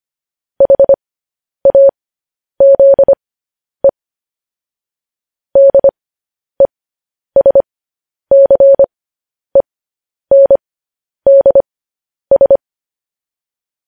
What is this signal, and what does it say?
Morse code, heard as a keyed tone, about 25 wpm, spaced as for 10 wpm, 550 Hz. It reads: HAZE DESCENDS